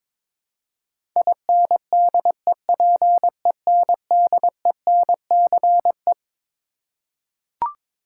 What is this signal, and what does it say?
Morse code, heard as a keyed tone, about 22 wpm, 700 Hz. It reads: INDEPENDENCE
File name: Twelve Letter Words - 22wpm